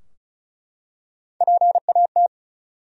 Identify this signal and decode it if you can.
Morse code, heard as a keyed tone, about 35 wpm, 700 Hz. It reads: PAT
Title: USA Common Names - Max 5 Letters - 35wpm